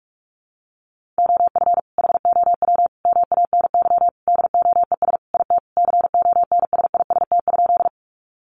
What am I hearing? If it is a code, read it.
Morse code, heard as a keyed tone, about 44 wpm, 700 Hz. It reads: OF HOW MANY DOES IT CONSIST?
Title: Sentences from Top 1400 Words 45wpm